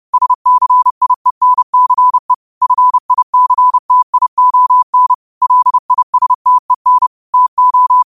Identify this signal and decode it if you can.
Morse code, heard as a keyed tone, about 30 wpm, 1000 Hz. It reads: SCIENCE FICTION LISTEN TO